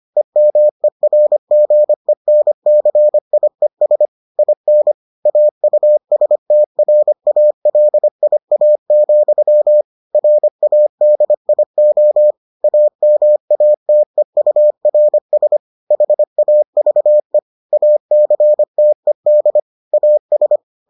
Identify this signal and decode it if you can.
Morse code, heard as a keyed tone, about 25 wpm, 600 Hz. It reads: EMERGENCIES IN AUSTRALIA, RADIO AMATEURS HAVE ACTED AS